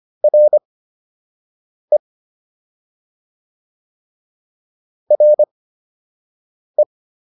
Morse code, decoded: RE RE

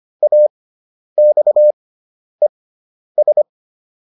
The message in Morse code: AXES

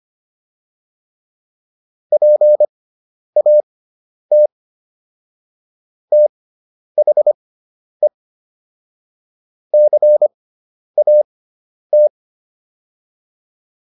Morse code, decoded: PAT THE CAT

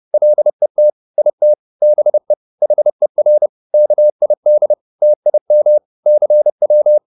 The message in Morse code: LET IT BE HER KID TIM CW

